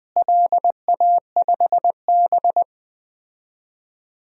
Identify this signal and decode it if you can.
Morse code, heard as a keyed tone, about 21 wpm, 700 Hz. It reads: LA5B
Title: DXCC-100 Call Signs - 4 Characters - Repeated 2x - 20wpm